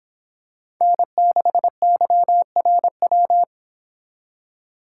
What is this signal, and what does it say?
Morse code, heard as a keyed tone, about 26 wpm, 700 Hz. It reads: N6YRW